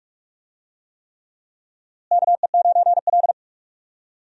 Morse code, heard as a keyed tone, about 46 wpm, 700 Hz. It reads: KE9L
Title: USA Call Signs - 4 Characters - Repeated 3x - 45wpm